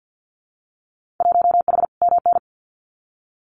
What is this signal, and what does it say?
Morse code, heard as a keyed tone, about 50 wpm, 700 Hz. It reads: 1H MN
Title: Winter Field Day Exchanges 50wpm